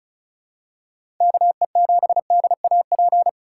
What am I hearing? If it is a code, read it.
Morse code, heard as a keyed tone, about 34 wpm, 700 Hz. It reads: KE7DAP
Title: USA Call Signs - 6 Characters - 2x Character Spacing - 35wpm